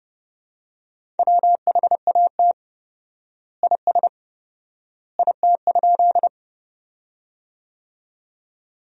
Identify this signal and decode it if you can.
Morse code, heard as a keyed tone, about 30 wpm, 700 Hz. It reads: WHAT IS IT?
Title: 4x Word-Spacing - Sentences from Top 100 Words - 30wpm